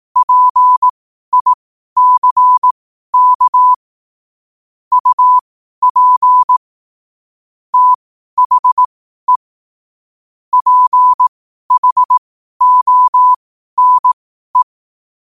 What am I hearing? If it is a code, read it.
Morse code, heard as a keyed tone, about 18 wpm, spaced as for 12 wpm, 1000 Hz. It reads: PICK UP THE PHONE
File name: qotdfiles / 12wpm